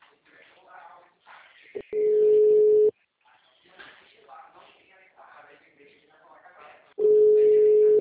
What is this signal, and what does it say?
Ringback tone (Europe, 425 Hz), with speech.